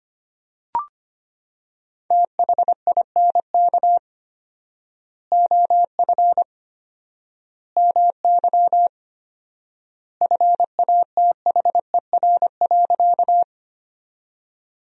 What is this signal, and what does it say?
Morse code, heard as a keyed tone, about 25 wpm, 700 Hz. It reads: THINK OF MY FATHER.
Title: 4x Word-Spacing - Sentences from Top 300 Words - 25wpm